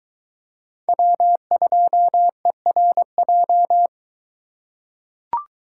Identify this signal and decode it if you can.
Morse code, heard as a keyed tone, about 22 wpm, 700 Hz. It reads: W2ERJ